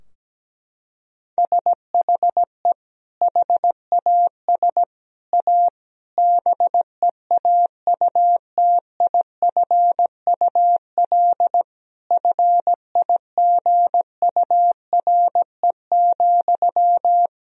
Morse code, heard as a keyed tone, about 17 wpm, 700 Hz. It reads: SHE HAS A BEAUTIFUL FIGURE,
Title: Sentences from Top 1000 Words - Review Part 2 - 17wpm